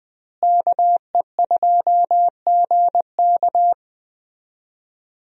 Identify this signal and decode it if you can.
Morse code, heard as a keyed tone, about 20 wpm, 700 Hz. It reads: KE2GK
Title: USA Call Signs - 5 Characters - Repeated 3x - 20wpm